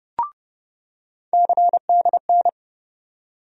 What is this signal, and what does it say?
Morse code, heard as a keyed tone, about 30 wpm, 700 Hz. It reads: CDN
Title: USA Call Sign Postfixes - 3 Letters 30wpm